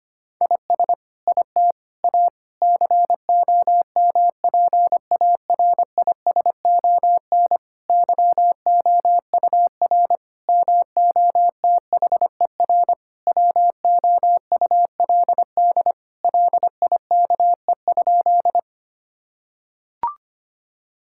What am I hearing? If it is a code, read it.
Morse code, heard as a keyed tone, about 25 wpm, 700 Hz. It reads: IS IT A COMPARISON YOUR MOTHER WOULD LIKE?
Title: Sentences from Top 1600 Words 25wpm